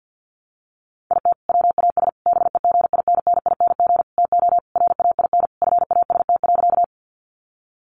Morse code, heard as a keyed tone, about 50 wpm, 700 Hz. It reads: IT WAS BEGINNING TO RAIN FAST.
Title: Sentences from Top 800 Words 50wpm